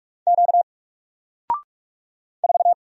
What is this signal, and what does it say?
Morse code, heard as a keyed tone, about 46 wpm, 700 Hz. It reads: Q 4
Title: Single Character (Letter:Number:Punctuation:Prosign) Alt 45wpm